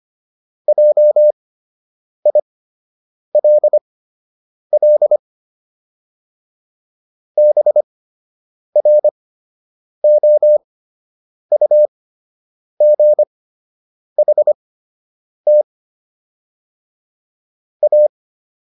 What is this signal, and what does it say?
Morse code, heard as a keyed tone, about 25 wpm, spaced as for 8 wpm, 600 Hz. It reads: JILL BROUGHT A